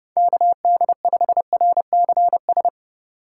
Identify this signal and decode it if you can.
Morse code, heard as a keyed tone, about 38 wpm, 700 Hz. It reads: KD5RCS
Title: Speed Racing - USA 2x3 Call Signs 30wpm